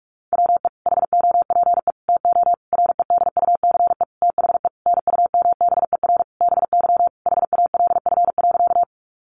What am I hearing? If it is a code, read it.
Morse code, heard as a keyed tone, about 44 wpm, 700 Hz. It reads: WE HOPE TO REDUCE THE NUMBER BY HALF.
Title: Sentences from Top 1100 Words 45wpm